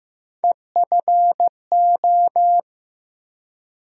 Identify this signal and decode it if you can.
Morse code, heard as a keyed tone, about 15 wpm, 700 Hz. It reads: EFO
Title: USA Call Sign Postfixes - 3 Letters 15wpm